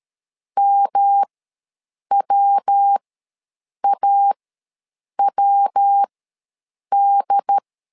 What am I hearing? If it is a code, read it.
Morse code, heard as a keyed tone, about 13 wpm, spaced as for 7 wpm, 800 Hz. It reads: MWAWD
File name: LEZIONI / 3 MWRO / 3 MW ws 12wpm 1816